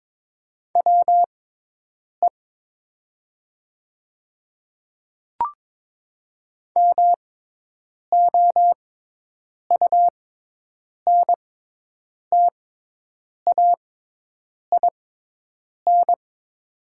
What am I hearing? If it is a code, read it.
Morse code, heard as a keyed tone, about 22 wpm, spaced as for 8 wpm, 700 Hz. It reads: WE MOUNTAIN
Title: 6x Character Spacing - Top 300 Words - 22wpm